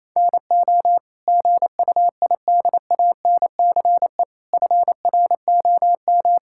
Morse code, heard as a keyed tone, about 27 wpm, 700 Hz. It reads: NO GUIDANCE FROM
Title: Sentences from Top 1600 Words 28wpm